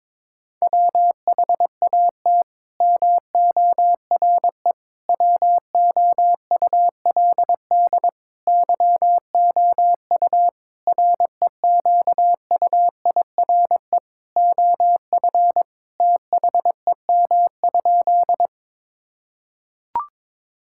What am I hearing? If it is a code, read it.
Morse code, heard as a keyed tone, about 22 wpm, 700 Hz. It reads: WHAT MORE WOULD YOU REQUIRE OF THEM?
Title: Sentences from Top 1000 Words - Review Part 6 - 22wpm